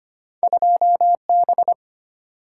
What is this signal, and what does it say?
Morse code, heard as a keyed tone, about 34 wpm, 700 Hz. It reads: 2B